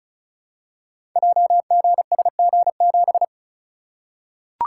Morse code, heard as a keyed tone, about 36 wpm, 700 Hz. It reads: JGSG7
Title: Code Groups - 5 Letter:Numbers - 35wpm